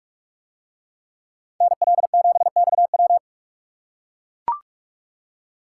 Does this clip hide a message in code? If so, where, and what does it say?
Yes: Morse code from 1.6 to 3.2 s, heard as a keyed tone, about 42 wpm, 700 Hz. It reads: NL7XW